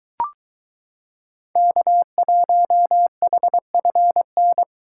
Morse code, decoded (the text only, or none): K1HFN